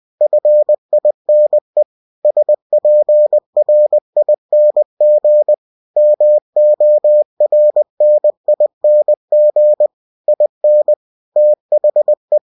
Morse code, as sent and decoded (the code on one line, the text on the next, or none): ..-. .. -. / ... .--. .-. .. -. --. / -- --- .-. -. .. -. --. / .. -. / - .... .
FINE SPRING MORNING IN THE